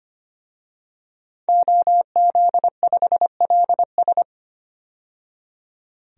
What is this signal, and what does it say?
Morse code, heard as a keyed tone, about 25 wpm, 700 Hz. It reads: OZ5LS
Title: DXCC-100 Call Signs - 5 Characters - 25wpm